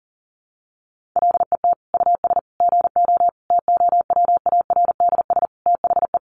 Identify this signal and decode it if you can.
Morse code, heard as a keyed tone, about 40 wpm, 700 Hz. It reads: LET US GO TOWARDS THE
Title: Sentences from Top 1100 Words 40wpm